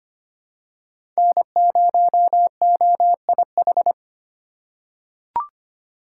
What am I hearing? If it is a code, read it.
Morse code, heard as a keyed tone, about 25 wpm, 700 Hz. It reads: N0OIH
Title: USA Call Signs - 5 Characters - 4x Character Spacing - 25wpm